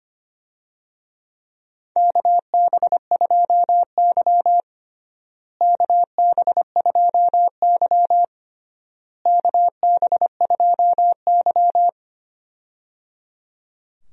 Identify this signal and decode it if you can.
Morse code, heard as a keyed tone, about 25 wpm, 700 Hz. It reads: KB2Y KB2Y KB2Y
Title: USA Call Signs - 4 Characters - Repeated 3x - 25wpm